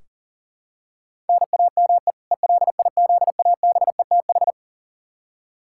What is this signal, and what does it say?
Morse code, heard as a keyed tone, about 40 wpm, 700 Hz. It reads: NAME ELIZABETH
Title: K1USN SST QSO Practice 40wpm